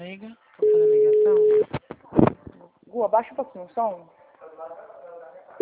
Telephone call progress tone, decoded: ringback tone

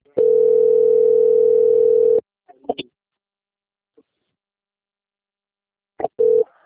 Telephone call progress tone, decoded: ringback tone

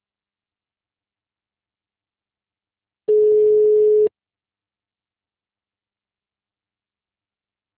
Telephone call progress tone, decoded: ringback tone

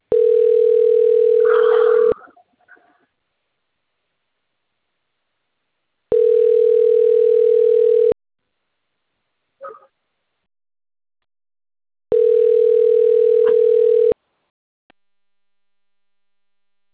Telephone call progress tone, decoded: ringback tone

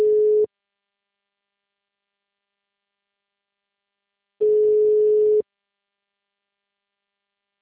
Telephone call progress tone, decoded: ringback tone

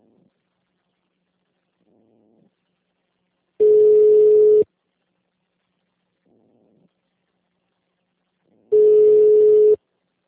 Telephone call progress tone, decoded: ringback tone

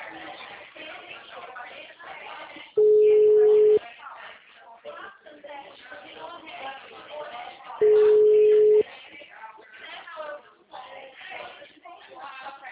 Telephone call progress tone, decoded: ringback tone

